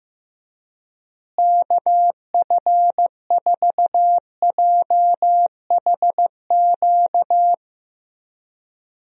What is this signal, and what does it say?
Morse code, heard as a keyed tone, about 15 wpm, 700 Hz. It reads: KF4JHQ
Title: USA Call Signs - 6 Characters - Repeated 3x - 15wpm